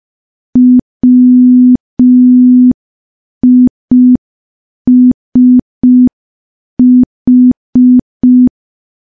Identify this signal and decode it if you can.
Morse code, heard as a keyed tone, about 5 wpm, 250 Hz. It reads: WISH
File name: morse / morse-training-05WPM